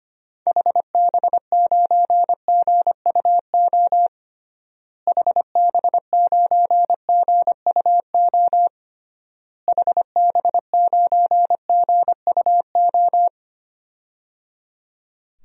Morse code, heard as a keyed tone, about 25 wpm, 700 Hz. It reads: HB9GUO HB9GUO HB9GUO
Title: DXCC-100 Call Signs - 6 Characters - Repeated 3x - 25wpm